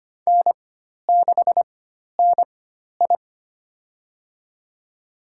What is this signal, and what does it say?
Morse code, heard as a keyed tone, about 25 wpm, spaced as for 12 wpm, 700 Hz. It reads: N6NI
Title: USA Call Signs - 4 Characters - 4x Character Spacing - 25wpm